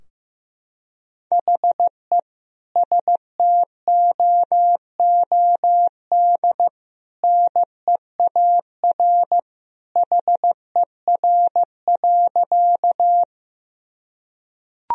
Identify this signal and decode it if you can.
Morse code, heard as a keyed tone, about 15 wpm, 700 Hz. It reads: HE STOOD NEAR HER.
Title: Sentences from Top 1000 Words - Review Part 2 - 15wpm